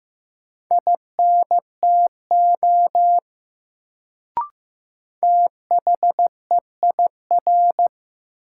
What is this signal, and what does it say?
Morse code, heard as a keyed tone, about 15 wpm, 700 Hz. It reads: INTO THEIR
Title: Sets of 2 Words - New Word Drill - 15wpm